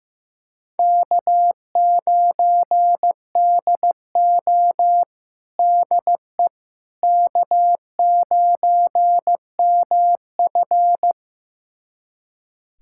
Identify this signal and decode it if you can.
Morse code, heard as a keyed tone, about 15 wpm, 700 Hz. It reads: K9DO DE K9MF
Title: USA Callsign DE Callsign - 4 Characters - 15wpm